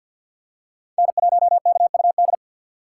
Morse code, heard as a keyed tone, about 47 wpm, 700 Hz. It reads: N1KUD
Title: USA Call Signs - 5 Characters - 3x Character Spacing - 50wpm